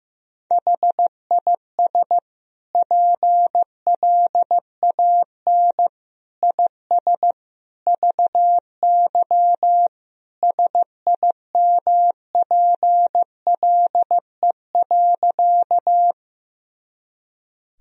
Morse code, heard as a keyed tone, about 15 wpm, 700 Hz. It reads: HIS PLAN IS VY SIMPLE.
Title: Sentences from Top 500 Words and Abbreviations 15wpm